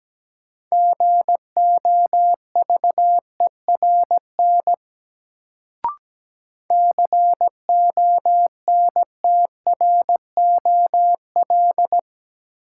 Morse code, heard as a keyed tone, about 17 wpm, 700 Hz. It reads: GOVERN CONTROL
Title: Sentences from Top 1000 Words - Review Part 2 - New Word Drill - 17wpm